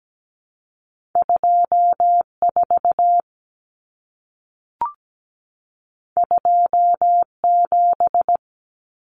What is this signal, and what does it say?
Morse code, heard as a keyed tone, about 17 wpm, 700 Hz. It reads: 24 27
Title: Numbers - 2-Digit 17wpm